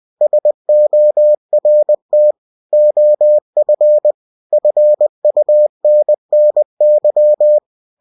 Morse code, heard as a keyed tone, about 20 wpm, 600 Hz. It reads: SORT OF FUNNY
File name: cw-books / winnie-the-pooh / winnie-20wpm-page-0049